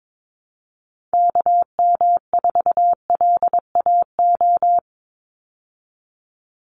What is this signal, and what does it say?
Morse code, heard as a keyed tone, about 22 wpm, 700 Hz. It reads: KM4LAO
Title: Call Signs Repeated 3x - 22wpm